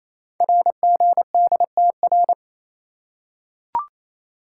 Morse code, heard as a keyed tone, about 28 wpm, 700 Hz. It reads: RGDTR
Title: Code Groups - 5 Letters - 28wpm